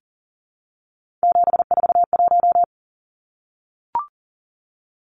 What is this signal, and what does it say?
Morse code, heard as a keyed tone, about 40 wpm, 700 Hz. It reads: 741
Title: Numbers - 3-Digit 40wpm